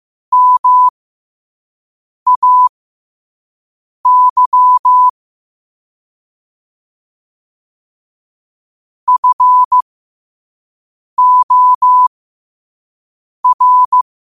Morse code, heard as a keyed tone, about 15 wpm, spaced as for 5 wpm, 1000 Hz. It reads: MAY FOR
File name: qotdfiles / 5wpm